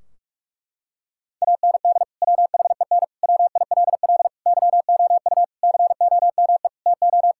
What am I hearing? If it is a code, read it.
Morse code, heard as a keyed tone, about 46 wpm, 700 Hz. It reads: AND WHEN WILL YOU COME TO
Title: Sentences from Top 1600 Words 45wpm